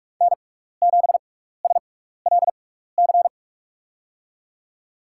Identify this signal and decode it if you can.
Morse code, heard as a keyed tone, about 47 wpm, spaced as for 16 wpm, 700 Hz. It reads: N7SLC
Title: USA Call Signs - 5 Characters - 6x Character Spacing - 45wpm